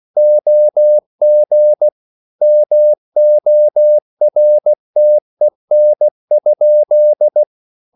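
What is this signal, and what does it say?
Morse code, heard as a keyed tone, about 16 wpm, 600 Hz. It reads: OG MORTEN?
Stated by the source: Slow (QRS) Morse code at 16 wpm, from CW med Gnister.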